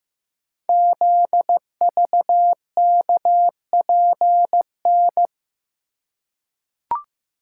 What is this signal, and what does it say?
Morse code, heard as a keyed tone, about 15 wpm, 700 Hz. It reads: ZVKPN
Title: Code Groups - 5 Letters - 15wpm